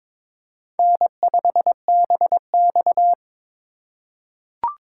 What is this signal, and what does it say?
Morse code, heard as a keyed tone, about 22 wpm, 700 Hz. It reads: N5BX
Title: USA Call Signs - 4 Characters - 4x Character Spacing - 22wpm